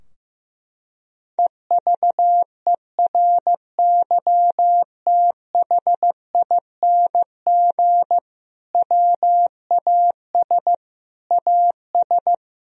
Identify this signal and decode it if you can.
Morse code, heard as a keyed tone, about 15 wpm, 700 Hz. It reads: EVERYTHING WAS AS
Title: Sentences from Top 1000 Words - Review Part 2 - 15wpm